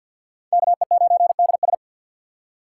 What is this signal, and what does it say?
Morse code, heard as a keyed tone, about 65 wpm, 700 Hz. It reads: KE9DS